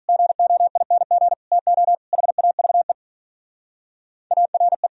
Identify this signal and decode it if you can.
Morse code, heard as a keyed tone, about 44 wpm, 700 Hz. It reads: GOING TO HAVE ARE